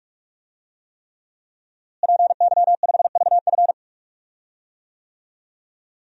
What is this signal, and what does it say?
Morse code, heard as a keyed tone, about 46 wpm, 700 Hz. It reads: PY5VF
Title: DXCC-100 Call Signs - 5 Characters - Repeated 3x - 45wpm